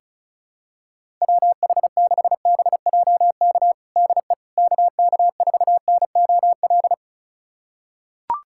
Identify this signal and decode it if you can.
Morse code, heard as a keyed tone, about 34 wpm, 700 Hz. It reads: WH6BJK DE KK4NOL
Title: USA Callsign DE Callsign 35wpm